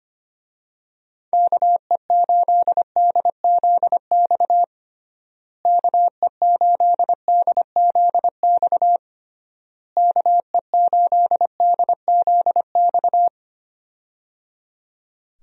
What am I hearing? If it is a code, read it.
Morse code, heard as a keyed tone, about 25 wpm, 700 Hz. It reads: KE8DZX KE8DZX KE8DZX
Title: USA Call Signs - 6 Characters - Repeated 3x - 25wpm